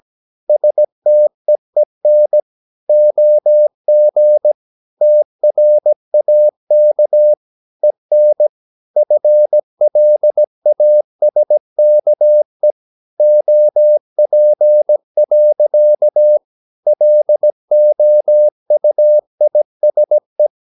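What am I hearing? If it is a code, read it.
Morse code, heard as a keyed tone, about 17 wpm, 600 Hz. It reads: STEEN OG TRAK EN FLASKE OP. LOUISE
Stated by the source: Slow (QRS) Morse practice at 17 wpm, from CW med Gnister.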